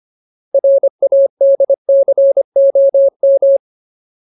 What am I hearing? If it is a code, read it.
Morse code, heard as a keyed tone, about 25 wpm, 550 Hz. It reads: RADCOM
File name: Ultimate Word List / 25WPM Ultimate